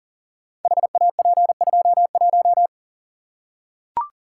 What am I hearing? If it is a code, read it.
Morse code, heard as a keyed tone, about 40 wpm, 700 Hz. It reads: HAP21
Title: Code Groups - 5 Letter:Numbers - 40wpm